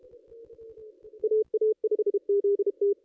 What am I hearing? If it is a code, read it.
Morse code, heard as a keyed tone, about 34 wpm, 440 Hz. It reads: AA5ZT
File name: contests / ss-cw-k5tr-op-n5rz -1900